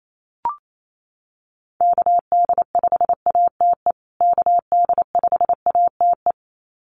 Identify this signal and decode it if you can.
Morse code, heard as a keyed tone, about 28 wpm, 700 Hz. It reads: KD5ATE KD5ATE
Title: USA Call Signs Repeated 2x 28wpm